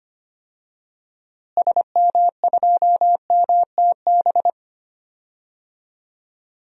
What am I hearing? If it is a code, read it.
Morse code, heard as a keyed tone, about 25 wpm, 700 Hz. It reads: SM2MTB
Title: DXCC-100 Call Signs - 6 Characters - Repeated 2x - 25wpm